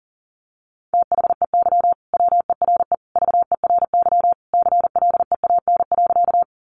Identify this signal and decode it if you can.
Morse code, heard as a keyed tone, about 40 wpm, 700 Hz. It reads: THEY WERE VERY CLEAN.
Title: Sentences from Top 1100 Words 40wpm